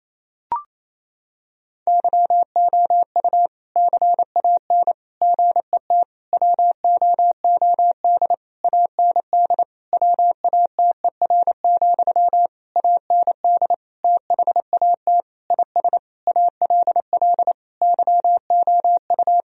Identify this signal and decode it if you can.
Morse code, heard as a keyed tone, about 28 wpm, 700 Hz. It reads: YOU CAN GET WOOD AND WATER, AND THAT IS ALL YOU
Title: Sentences from Top 1000 Words - Review Part 2 - 28wpm